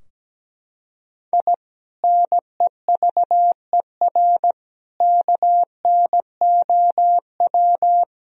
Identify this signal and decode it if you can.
Morse code, heard as a keyed tone, about 17 wpm, 700 Hz. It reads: I NEVER KNOW
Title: Sentences from Top 1000 Words - Review Part 3 - 17wpm